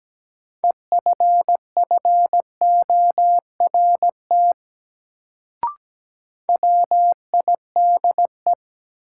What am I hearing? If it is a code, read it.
Morse code, heard as a keyed tone, about 17 wpm, 700 Hz. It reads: EFFORT WIDE
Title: Sets of 4 Words - New Word Drill - 17wpm